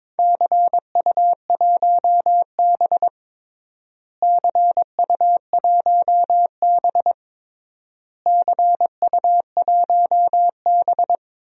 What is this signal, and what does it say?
Morse code, heard as a keyed tone, about 22 wpm, 700 Hz. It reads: CU1B CU1B CU1B
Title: DXCC-100 Call Signs - 4 Characters - Repeated 3x - 22wpm